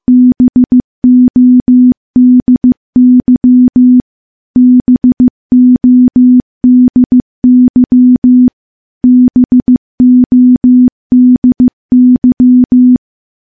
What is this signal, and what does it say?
Morse code, heard as a keyed tone, about 15 wpm, 250 Hz. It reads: BODY BODY BODY
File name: morse-training-15WPM